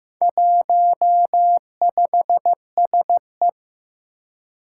Morse code, heard as a keyed tone, about 15 wpm, 700 Hz. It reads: 15SE
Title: Four Challenging Letter-Number Combinations 15wpm